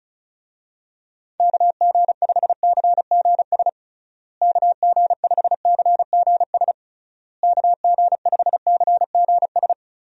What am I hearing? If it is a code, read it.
Morse code, heard as a keyed tone, about 36 wpm, 700 Hz. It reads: KG5CGS KG5CGS KG5CGS
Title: USA Call Signs - 6 Characters - Repeated 3x - 35wpm